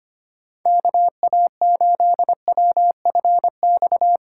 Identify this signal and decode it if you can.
Morse code, heard as a keyed tone, about 32 wpm, 700 Hz. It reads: KA8WFX